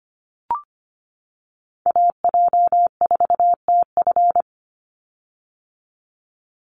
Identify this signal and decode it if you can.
Morse code, heard as a keyed tone, about 25 wpm, 700 Hz. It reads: AJ4TF
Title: USA Call Signs 25wpm